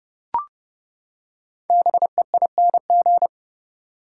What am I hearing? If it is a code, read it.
Morse code, heard as a keyed tone, about 29 wpm, 700 Hz. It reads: BEING
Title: Five Letter Words - 30wpm